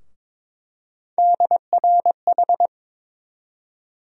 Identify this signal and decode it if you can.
Morse code, heard as a keyed tone, about 22 wpm, 700 Hz. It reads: DRH
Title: USA Call Sign Postfixes - 3 Letters 22wpm